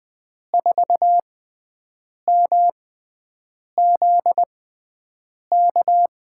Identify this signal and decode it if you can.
Morse code, heard as a keyed tone, about 20 wpm, spaced as for 7 wpm, 700 Hz. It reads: 4MZK
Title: USA Call Signs - 5 Characters - 6x Character Spacing - 20wpm